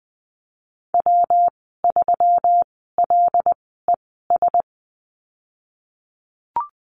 Morse code, heard as a keyed tone, about 20 wpm, spaced as for 15 wpm, 700 Hz. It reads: W3LES
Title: 128 Lesson 21g - Number 6 - Call Signs 20wpm